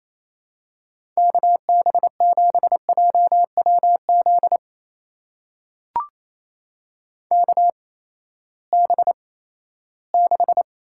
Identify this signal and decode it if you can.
Morse code, heard as a keyed tone, about 28 wpm, spaced as for 16 wpm, 700 Hz. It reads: KB7JWZ KB6
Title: USA Call Signs - 6 Characters - 8x Character Spacing - 28wpm